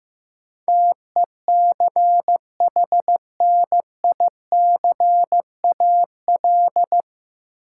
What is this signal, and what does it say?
Morse code, heard as a keyed tone, about 15 wpm, 700 Hz. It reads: TECHNICAL
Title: Sentences from Top 1100 Words - New Word Drill - 15wpm